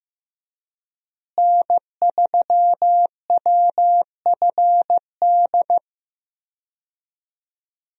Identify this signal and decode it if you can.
Morse code, heard as a keyed tone, about 15 wpm, 700 Hz. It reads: N3WFD
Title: USA Call Signs - 5 Characters - Repeated 2x - 15wpm